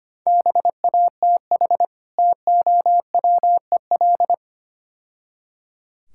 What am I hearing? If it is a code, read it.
Morse code, heard as a keyed tone, about 25 wpm, 700 Hz. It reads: BATH TOWEL
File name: practice / 25wpm / Difficult Words - Speed-Racing with Hints - 25wpm